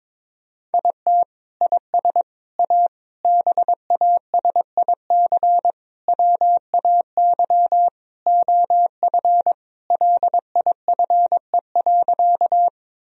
Morse code, heard as a keyed tone, about 22 wpm, 700 Hz. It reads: IT IS A BASIC WAY OF LIFE.
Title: Sentences from Top 1000 Words - Review Part 3 - 22wpm